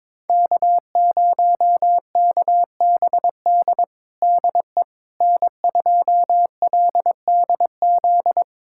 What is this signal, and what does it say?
Morse code, heard as a keyed tone, about 22 wpm, 700 Hz. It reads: K0KBD DE N2LDZ
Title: USA Callsign DE Callsign - 5 Characters - 22wpm